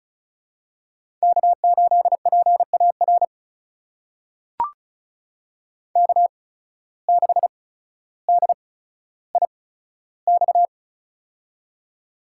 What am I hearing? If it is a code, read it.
Morse code, heard as a keyed tone, about 36 wpm, spaced as for 15 wpm, 700 Hz. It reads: K8PAR K6DIX
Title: USA Call Signs - 5 Characters - 8x Character Spacing - 35wpm